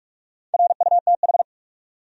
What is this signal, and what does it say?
Morse code, heard as a keyed tone, about 41 wpm, 700 Hz. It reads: RUTH
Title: USA Common Names - Max 5 Letters - 45wpm